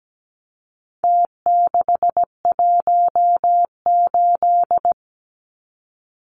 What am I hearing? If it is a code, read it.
Morse code, heard as a keyed tone, about 17 wpm, 700 Hz. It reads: T618
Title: Numbers - Contest Style - 4-Digit 17wpm